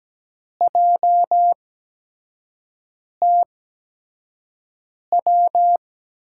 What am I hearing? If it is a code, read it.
Morse code, heard as a keyed tone, about 17 wpm, spaced as for 5 wpm, 700 Hz. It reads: JTW
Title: USA Call Signs - 8x Character Spacing - 17wpm